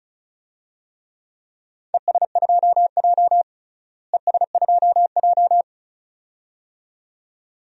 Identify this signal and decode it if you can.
Morse code, heard as a keyed tone, about 34 wpm, 690 Hz. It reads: ES2J ES2J
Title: DXCC-100 Call Signs - 4 Characters - Repeated 2x - 35wpm